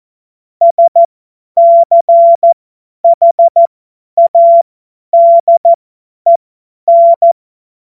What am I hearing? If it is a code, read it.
Morse code, heard as a keyed tone, about 14 wpm, spaced as for 10 wpm, 680 Hz. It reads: SCHADEN
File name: Morsekurs / Witze / Witz 13 Februar 2024 50 F70